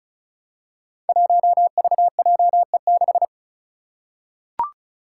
Morse code, heard as a keyed tone, about 36 wpm, 700 Hz. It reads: 1VJE6
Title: Five Challenging Letter-Number Combinations 35wpm